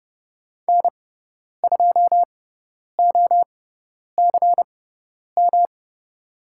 Morse code, heard as a keyed tone, about 30 wpm, spaced as for 10 wpm, 700 Hz. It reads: N2OCM
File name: Lesson 18g - Letter m - Call Signs 30wpm